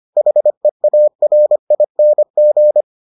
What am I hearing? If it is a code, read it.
Morse code, heard as a keyed tone, about 25 wpm, 600 Hz. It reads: HEARING